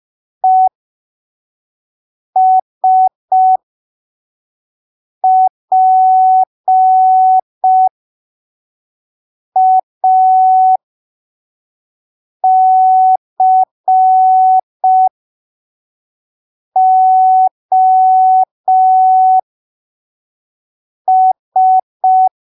Morse code, heard as a keyed tone, about 5 wpm, spaced as for 3 wpm, 750 Hz. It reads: ESPACOS